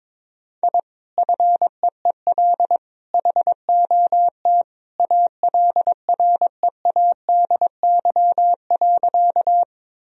Morse code, heard as a keyed tone, about 22 wpm, 690 Hz. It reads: I FEEL HOT ALREADY.
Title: Sentences from Top 1000 Words - Review Part 3 - 22wpm